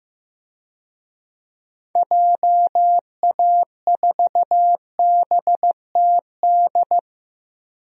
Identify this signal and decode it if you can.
Morse code, heard as a keyed tone, about 15 wpm, 700 Hz. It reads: JA4BTD
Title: DXCC-100 Call Signs - 6 Characters - 15wpm